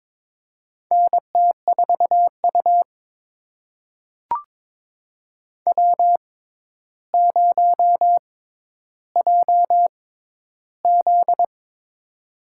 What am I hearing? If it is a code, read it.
Morse code, heard as a keyed tone, about 22 wpm, spaced as for 11 wpm, 700 Hz. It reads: NT4U W0JZ
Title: USA Call Signs - 6x Character Spacing - 22wpm